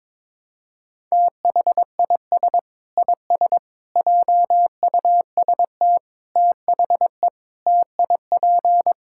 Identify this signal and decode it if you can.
Morse code, heard as a keyed tone, about 21 wpm, 700 Hz. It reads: THIS IS JUST THE TIP
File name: practice / 22wpm / Sets of 8 Words - 22wpm